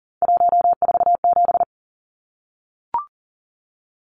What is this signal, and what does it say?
Morse code, heard as a keyed tone, about 37 wpm, 700 Hz. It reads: 147
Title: Numbers - 3-Digit 40wpm